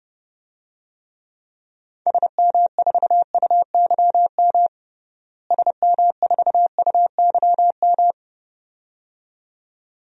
Morse code, heard as a keyed tone, about 30 wpm, 700 Hz. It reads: SM4UYM SM4UYM
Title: DXCC-100 Call Signs - 6 Characters - Repeated 2x - 30wpm